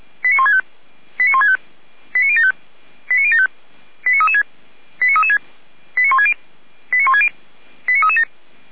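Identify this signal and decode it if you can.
Five-tone selective call (ZVEI), heard as quick runs of five tones, tones of about 70 ms. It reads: 89165 89165 89075 89075 89336 89336 89160 89160 89337
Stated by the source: Five-tone ZVEI selective call received by SDR.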